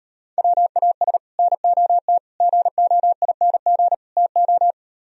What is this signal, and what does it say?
Morse code, heard as a keyed tone, about 38 wpm, 700 Hz. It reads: WAS NOT GOING TO